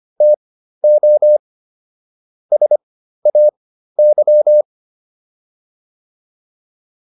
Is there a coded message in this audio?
Yes: Morse code from 0.2 to 4.6 s, heard as a keyed tone, about 25 wpm, spaced as for 13 wpm, 600 Hz. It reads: TO SAY